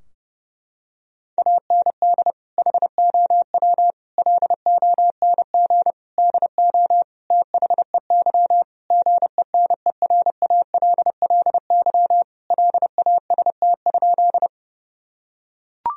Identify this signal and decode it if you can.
Morse code, heard as a keyed tone, about 30 wpm, 700 Hz. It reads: AND HOW LONG DO THEY GENERALLY LAST?
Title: Sentences from Top 1000 Words - Review Part 4 - 30wpm